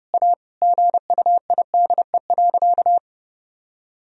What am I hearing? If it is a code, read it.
Morse code, heard as a keyed tone, about 30 wpm, 700 Hz. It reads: A GUIDE.
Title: Sentences from Top 1000 Words - Review Part 3 - 30wpm